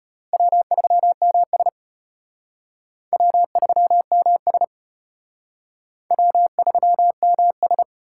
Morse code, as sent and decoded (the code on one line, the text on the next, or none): .-- ...-- -- ... / .-- ...-- -- ... / .-- ...-- -- ...
W3MS W3MS W3MS